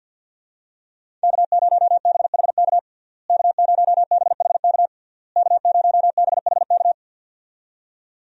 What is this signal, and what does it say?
Morse code, heard as a keyed tone, about 48 wpm, 700 Hz. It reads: K0BHK K0BHK K0BHK
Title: USA Call Signs - 5 Characters - Repeated 3x - 50wpm